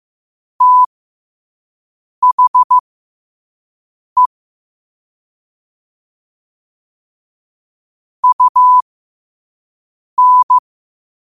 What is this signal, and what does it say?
Morse code, heard as a keyed tone, about 15 wpm, spaced as for 5 wpm, 1000 Hz. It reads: THE UN
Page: 5 WPM morse code quotes